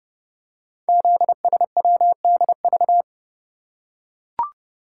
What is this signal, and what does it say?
Morse code, heard as a keyed tone, about 30 wpm, 700 Hz. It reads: ZSWDV
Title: Code Groups - 5 Letters - 30wpm